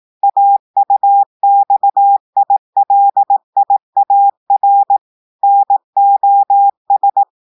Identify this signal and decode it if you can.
Morse code, heard as a keyed tone, about 18 wpm, 800 Hz. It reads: AUXILIAR NOS
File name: audios / portugais / portugais texte en clair 1 vitesse 18 mots minutes